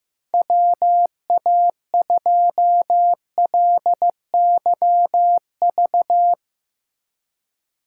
Morse code, heard as a keyed tone, about 15 wpm, 700 Hz. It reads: WA2LYV
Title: USA Call Signs - 3x Character Spacing - 15wpm